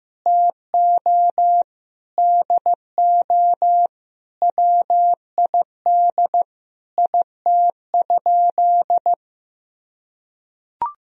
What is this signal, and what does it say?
Morse code, heard as a keyed tone, about 15 wpm, 700 Hz. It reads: TO DO WID IT?
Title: Sentences from Top 100 Words and Abbreviations 15wpm